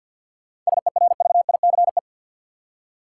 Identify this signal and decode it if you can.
Morse code, heard as a keyed tone, about 47 wpm, 700 Hz. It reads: SERVICE
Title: Seven Letter Words - 50wpm